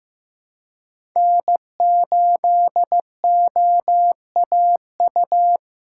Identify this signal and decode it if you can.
Morse code, heard as a keyed tone, about 15 wpm, 700 Hz. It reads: N8OAU
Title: USA Callsign DE Callsign - 5 Characters - 15wpm